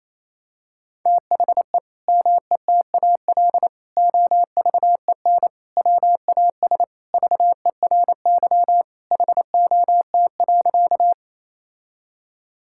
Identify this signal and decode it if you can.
Morse code, heard as a keyed tone, about 28 wpm, 700 Hz. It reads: THE METAL OVEN WAS VERY HOT.
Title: Sentences from Top 1000 Words - Review Part 6 - 28wpm